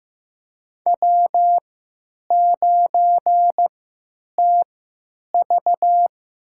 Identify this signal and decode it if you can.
Morse code, heard as a keyed tone, about 15 wpm, spaced as for 9 wpm, 700 Hz. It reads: W9TV
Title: USA Call Signs - 4 Characters - 3x Character Spacing - 15wpm